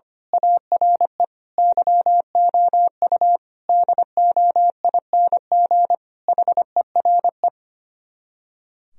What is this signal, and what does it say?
Morse code, heard as a keyed tone, about 24 wpm, 700 Hz. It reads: ARE YOU DOING HERE
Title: Speed-Racing - Sets of 5 Words 25wpm